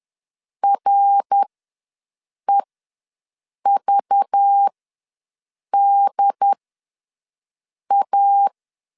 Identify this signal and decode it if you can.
Morse code, heard as a keyed tone, about 10 wpm, spaced as for 6 wpm, 800 Hz. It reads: REVDA
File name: LEZIONI / 5 LBVC / 5 LBV ws 10wpm 3645